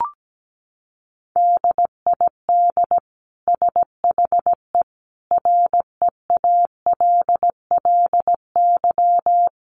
Morse code, heard as a keyed tone, about 17 wpm, 700 Hz. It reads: DID SHE REALLY
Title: Sentences from Top 1400 Words 17wpm